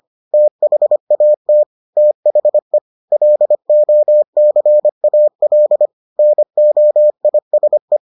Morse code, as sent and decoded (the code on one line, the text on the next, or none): - .... .- - / - .... / .-.. --- -.-. .- .-.. / -. --- .. ... .
THAT THE LOCAL NOISE